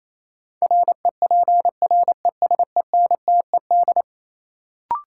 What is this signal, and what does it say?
Morse code, heard as a keyed tone, about 28 wpm, 700 Hz. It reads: REPRESENTED
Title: Sets of 5 Words - New Word Drill - 28wpm